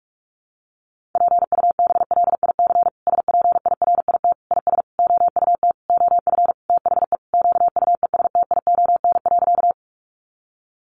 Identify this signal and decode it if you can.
Morse code, heard as a keyed tone, about 44 wpm, 700 Hz. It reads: PUBLIC SPIRIT IS OUT OF THE QUESTION.
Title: Sentences from Top 1400 Words 45wpm